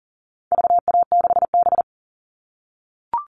Morse code, heard as a keyed tone, about 40 wpm, 700 Hz. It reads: VA6B
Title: All Cut Numbers - 4-Digit 40wpm